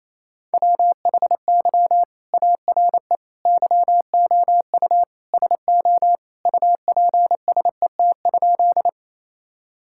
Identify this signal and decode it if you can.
Morse code, heard as a keyed tone, about 27 wpm, 700 Hz. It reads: WHY ARE YOU SO UPSET?
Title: Sentences from Top 1000 Words - Review Part 5 - 28wpm